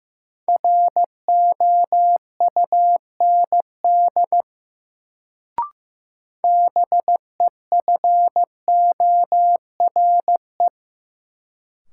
Morse code, heard as a keyed tone, about 15 wpm, 700 Hz. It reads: ROUND BEFORE
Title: Sentences from Top 200 Words - New Word Drill - 15wpm